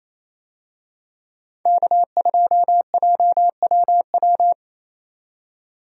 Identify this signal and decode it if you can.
Morse code, heard as a keyed tone, about 28 wpm, 700 Hz. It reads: K2JWW
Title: USA Call Signs - 5 Characters - 28wpm